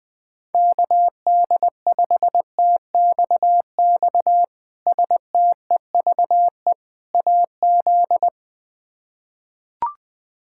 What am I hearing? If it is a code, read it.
Morse code, heard as a keyed tone, about 20 wpm, 700 Hz. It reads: KD5TXX STEVE AZ
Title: K1USN SST QSO Practice - 20wpm